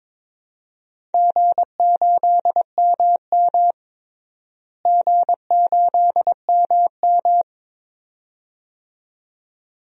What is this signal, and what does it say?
Morse code, heard as a keyed tone, about 22 wpm, 700 Hz. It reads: G8MM G8MM
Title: DXCC-100 Call Signs - 4 Characters - Repeated 2x - 22wpm